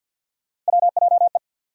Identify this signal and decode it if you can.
Morse code, heard as a keyed tone, about 51 wpm, 700 Hz. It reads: WJE
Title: USA Call Sign Postfixes - 3 Letters 50wpm